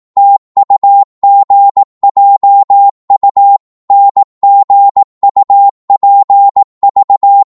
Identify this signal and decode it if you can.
Morse code, heard as a keyed tone, about 18 wpm, 800 Hz. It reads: TUGJU NGUPV